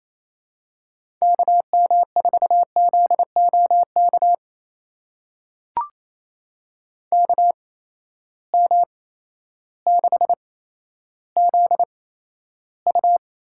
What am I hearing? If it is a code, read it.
Morse code, heard as a keyed tone, about 28 wpm, 700 Hz. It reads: K M 4 Z O K K M 6 Z U
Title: USA Call Signs - 6 Characters - 8x Character Spacing - 28wpm